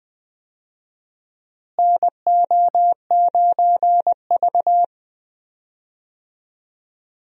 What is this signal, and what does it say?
Morse code, heard as a keyed tone, about 20 wpm, 700 Hz. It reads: NO9V
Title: USA Call Signs - 4 Characters - Repeated 2x - 20wpm